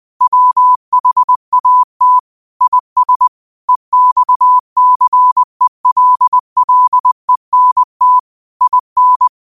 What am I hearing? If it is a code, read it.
Morse code, heard as a keyed tone, about 20 wpm, 1000 Hz. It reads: WHAT IS EXCELLENT IN
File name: qotdfiles / 20wpm